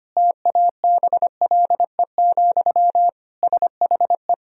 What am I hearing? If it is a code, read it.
Morse code, heard as a keyed tone, about 25 wpm, 700 Hz. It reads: TABLE, SHE
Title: Sentences from Top 1000 Words - Review Part 2 - 25wpm